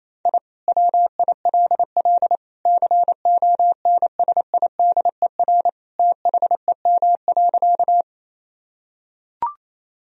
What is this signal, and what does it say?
Morse code, heard as a keyed tone, about 28 wpm, 700 Hz. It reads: I WILL CONSIDER THEM.
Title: Sentences from Top 1000 Words - Review Part 4 - 28wpm